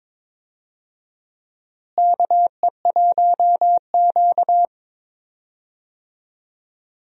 Morse code, heard as a keyed tone, about 21 wpm, 700 Hz. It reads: KE1Q
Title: USA Call Signs - 4 Characters - Repeated 2x - 22wpm